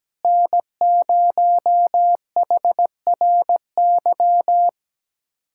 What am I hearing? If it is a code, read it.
Morse code, heard as a keyed tone, about 17 wpm, 700 Hz. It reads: N0HRY
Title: USA Callsign DE Callsign - 5 Characters - 17wpm